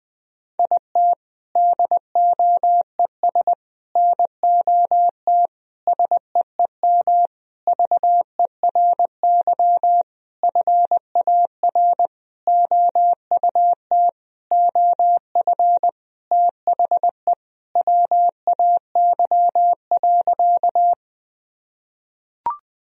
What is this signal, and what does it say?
Morse code, heard as a keyed tone, about 21 wpm, 700 Hz. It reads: IT DOES NOT SEEM VERY FAR OUT OF THE WAY.
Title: Sentences from Top 1000 Words - Review Part 1 - 20wpm